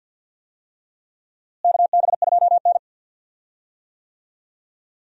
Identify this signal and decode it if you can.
Morse code, heard as a keyed tone, about 50 wpm, 700 Hz. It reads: KB2N
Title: USA Call Signs - 4 Characters - Repeated 3x - 50wpm